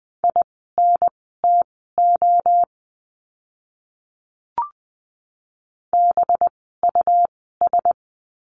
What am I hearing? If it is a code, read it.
Morse code, heard as a keyed tone, about 20 wpm, spaced as for 14 wpm, 700 Hz. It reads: INTO BUS
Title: 185 Lesson 30f - Letter b - Words 20wpm